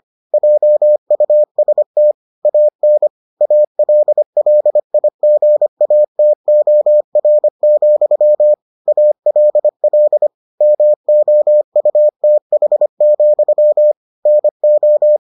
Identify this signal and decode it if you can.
Morse code, heard as a keyed tone, about 25 wpm, 600 Hz. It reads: JUST AN ALLIGATOR, ALL MOUTH, NO